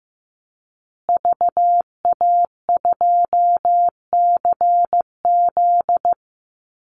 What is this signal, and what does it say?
Morse code, heard as a keyed tone, about 15 wpm, 700 Hz. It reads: VA2CZ
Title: Call Signs Repeated 3x - 15wpm